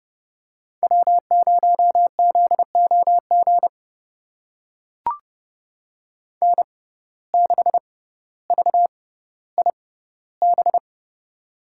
Morse code, heard as a keyed tone, about 32 wpm, spaced as for 16 wpm, 700 Hz. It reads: W0ZOG N6VIB
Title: USA Call Signs - 5 Characters - 6x Character Spacing - 30wpm